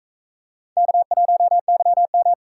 Morse code, heard as a keyed tone, about 42 wpm, 700 Hz. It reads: K1YM